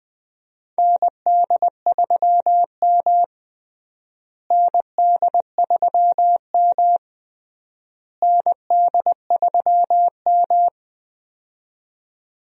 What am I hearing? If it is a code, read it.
Morse code, heard as a keyed tone, about 20 wpm, 700 Hz. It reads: ND3M ND3M ND3M
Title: USA Call Signs - 4 Characters - Repeated 3x - 20wpm